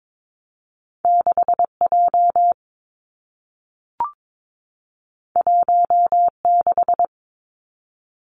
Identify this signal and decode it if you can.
Morse code, heard as a keyed tone, about 22 wpm, 700 Hz. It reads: 6 J 1 6
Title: Two Challenging Letter-Number Combinations 22wpm